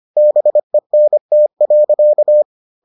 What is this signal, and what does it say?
Morse code, heard as a keyed tone, about 25 wpm, 600 Hz. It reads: BENT.